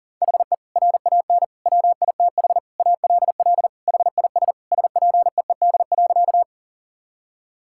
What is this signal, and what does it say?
Morse code, heard as a keyed tone, about 37 wpm, 700 Hz. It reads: HE RAN WITH ALL HIS SPEED.
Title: Sentences from Top 1000 Words - Review Part 5 - 40wpm